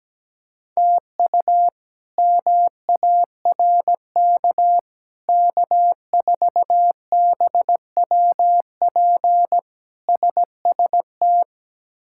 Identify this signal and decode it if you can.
Morse code, heard as a keyed tone, about 17 wpm, 700 Hz. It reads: TU MARK K4BWP SST
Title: K1USN SST QSO Practice - 17wpm